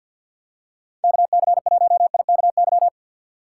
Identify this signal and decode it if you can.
Morse code, heard as a keyed tone, about 53 wpm, 700 Hz. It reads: KC1IKY
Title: USA Call Signs - 6x Character Spacing - 50wpm